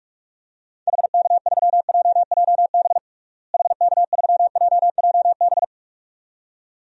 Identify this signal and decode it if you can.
Morse code, heard as a keyed tone, about 46 wpm, 700 Hz. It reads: HK3JJB HK3JJB
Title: DXCC-100 Call Signs - 6 Characters - Repeated 2x - 45wpm